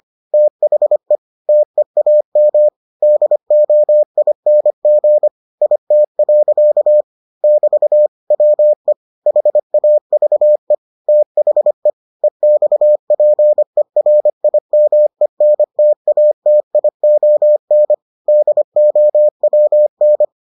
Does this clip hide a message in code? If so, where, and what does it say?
Yes: Morse code from 0.3 to 20.2 s, heard as a keyed tone, about 25 wpm, 600 Hz. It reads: THE TEAM DOING IT. <BT> WE HAVE THE EXPERIMENTATION DOWN